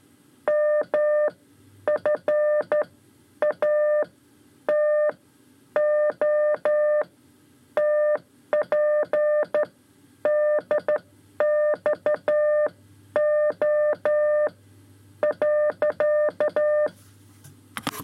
Morse code, heard as a keyed tone, about 12 wpm, spaced as for 9 wpm, 600 Hz. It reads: MFATOTPDXO.
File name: 5 Lezione CW 2